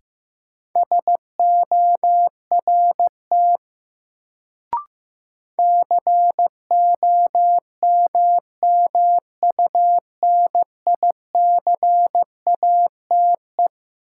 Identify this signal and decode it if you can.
Morse code, heard as a keyed tone, about 15 wpm, 700 Hz. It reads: SORT COMMUNICATE
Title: Sentences from Top 1300 Words - New Word Drill Alt - 15wpm